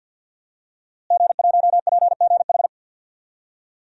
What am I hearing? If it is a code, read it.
Morse code, heard as a keyed tone, about 52 wpm, 700 Hz. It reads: G1PGH
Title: DXCC-100 Call Signs - 5 Characters - Repeated 2x - 50wpm